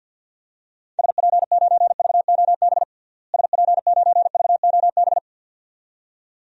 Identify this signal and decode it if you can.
Morse code, heard as a keyed tone, about 52 wpm, 700 Hz. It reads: SP9VOB SP9VOB
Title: DXCC-100 Call Signs - 6 Characters - Repeated 2x - 50wpm